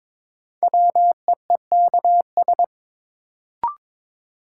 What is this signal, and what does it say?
Morse code, heard as a keyed tone, about 23 wpm, 700 Hz. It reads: WEEKS
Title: Sets of 5 Words - New Word Drill - 22wpm